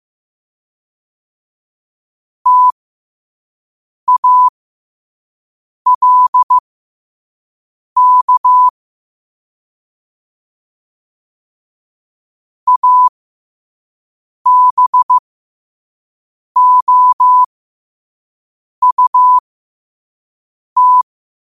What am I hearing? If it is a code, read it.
Morse code, heard as a keyed tone, about 15 wpm, spaced as for 5 wpm, 1000 Hz. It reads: TALK ABOUT
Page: Quotes for Wed, 13 Aug 2025 in Morse Code at 5 words per minute.